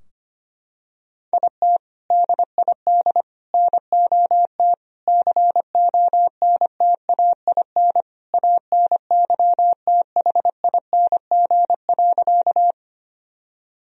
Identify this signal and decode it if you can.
Morse code, heard as a keyed tone, about 25 wpm, 700 Hz. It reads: IT DID NOT CONTAIN ANYTHING.
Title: Sentences from Top 1000 Words - Review Part 2 - 25wpm